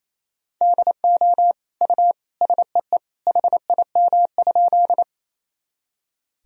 Morse code, heard as a keyed tone, about 28 wpm, 700 Hz. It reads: DO U SEE HIM?
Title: Sentences from Top 100 Words and Abbreviations 28wpm